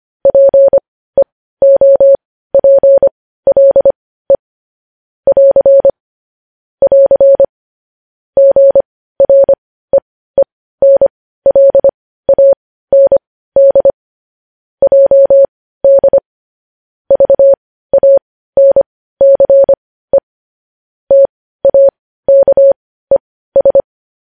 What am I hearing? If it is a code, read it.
Morse code, heard as a keyed tone, about 25 wpm, spaced as for 15 wpm, 550 Hz. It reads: PEOPLE <AR> <AR> GREENLAND JD VANCE TAKES
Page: News Headlines in Morse Code at 15 WPM